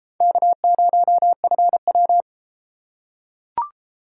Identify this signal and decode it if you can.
Morse code, heard as a keyed tone, about 31 wpm, 700 Hz. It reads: K0FW